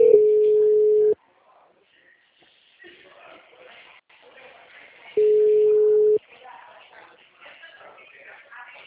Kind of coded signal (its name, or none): telephone call progress tone